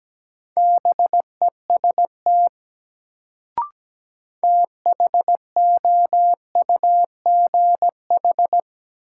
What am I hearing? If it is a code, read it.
Morse code, heard as a keyed tone, about 17 wpm, 700 Hz. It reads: BEST THOUGH
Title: Sentences from Top 400 Words - New Word Drill - 17wpm